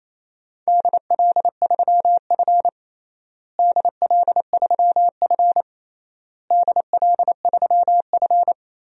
Morse code, heard as a keyed tone, about 28 wpm, 700 Hz. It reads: DL3F DL3F DL3F
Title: DXCC-100 Call Signs - 4 Characters - Repeated 3x - 28wpm